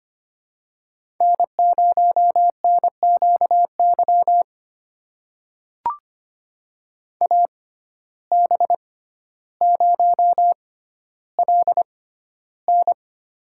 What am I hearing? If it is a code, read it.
Morse code, heard as a keyed tone, about 25 wpm, spaced as for 13 wpm, 700 Hz. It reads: N0NQY AB0LN
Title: USA Call Signs - 5 Characters - 6x Character Spacing - 25wpm